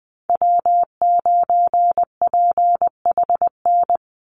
Morse code, heard as a keyed tone, about 20 wpm, 700 Hz. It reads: W9PHN
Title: USA Call Signs 20wpm